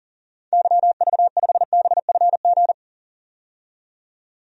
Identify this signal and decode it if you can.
Morse code, heard as a keyed tone, about 40 wpm, 700 Hz. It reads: YV5BFG
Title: DXCC-100 Call Signs - 6 Characters - 40wpm